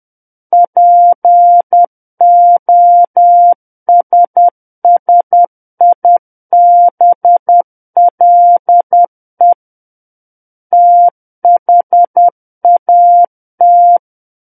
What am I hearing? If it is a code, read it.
Morse code, heard as a keyed tone, about 10 wpm, 700 Hz. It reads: POSSIBLE THAT
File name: Book One / 01-Chapter 1 - 10 WPM